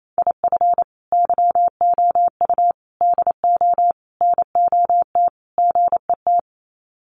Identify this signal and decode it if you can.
Morse code, heard as a keyed tone, about 28 wpm, 700 Hz. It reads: IF YOU DO NOT GET
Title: Sets of 5 Words Encore2 28wpm